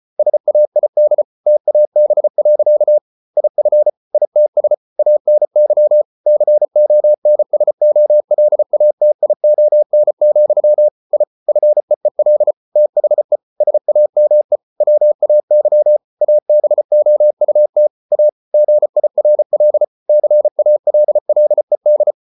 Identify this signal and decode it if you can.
Morse code, heard as a keyed tone, about 34 wpm, 600 Hz. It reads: SAID TAB. IF ITS ANY CONSOLATION, I FEEL THE SAME WAY ABOUT A GIRL CALLED